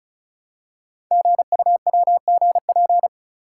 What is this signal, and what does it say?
Morse code, heard as a keyed tone, about 36 wpm, 700 Hz. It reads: GUWGP
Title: Code Groups - 5 Letters - 35wpm